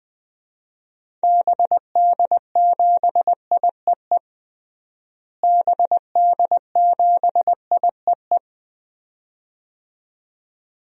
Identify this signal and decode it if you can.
Morse code, heard as a keyed tone, about 21 wpm, 700 Hz. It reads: BD7IEE BD7IEE
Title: DXCC-100 Call Signs - 6 Characters - Repeated 2x - 20wpm